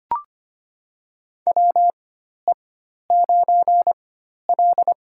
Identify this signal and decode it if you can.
Morse code, heard as a keyed tone, about 25 wpm, spaced as for 12 wpm, 700 Hz. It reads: WE9L
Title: USA Call Signs - 4x Character Spacing - 25wpm